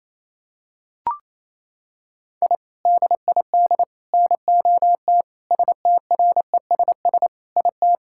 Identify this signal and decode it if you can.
Morse code, heard as a keyed tone, about 27 wpm, 700 Hz. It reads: I DID NOT STRESS IT
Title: Sentences from Top 1000 Words - Review Part 6 - 28wpm